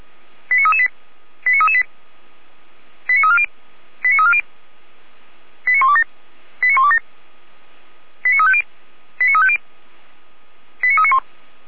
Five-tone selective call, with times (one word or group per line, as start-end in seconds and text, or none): 0.5-0.9 89338
1.5-1.8 89338
3.1-3.5 89340
4.0-4.4 89340
5.7-6.0 89127
6.6-7.0 89127
8.2-8.6 89350
9.2-9.6 89350
10.8-11.2 89391